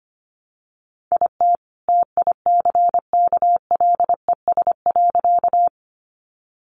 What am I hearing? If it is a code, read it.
Morse code, heard as a keyed tone, about 25 wpm, 700 Hz. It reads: IT TICKLES.
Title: Additional Easy Phrases 25wpm